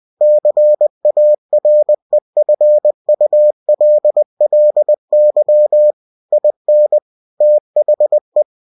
Morse code, heard as a keyed tone, about 20 wpm, 600 Hz. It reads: CAREFULLY IN THE
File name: cw-books / winnie-the-pooh / winnie-20wpm-page-0058